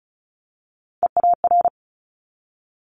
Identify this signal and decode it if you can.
Morse code, heard as a keyed tone, about 36 wpm, 700 Hz. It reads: EAR
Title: Multi-letter Phonemes 35wpm